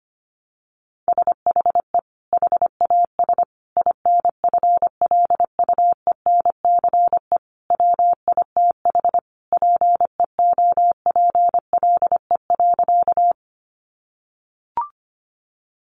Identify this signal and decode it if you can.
Morse code, heard as a keyed tone, about 25 wpm, 700 Hz. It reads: SHE HAS INFLUENCE WITH PEOPLE.
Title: Sentences from Top 1100 Words 25wpm